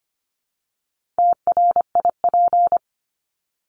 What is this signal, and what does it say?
Morse code, heard as a keyed tone, about 25 wpm, 700 Hz. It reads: TRIP
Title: Top 1251-1500 Words 25wpm